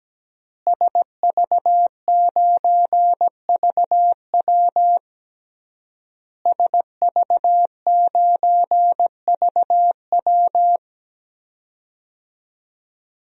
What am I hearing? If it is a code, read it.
Morse code, heard as a keyed tone, about 17 wpm, 700 Hz. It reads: SV9VW SV9VW
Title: DXCC-100 Call Signs - 5 Characters - Repeated 3x - 17wpm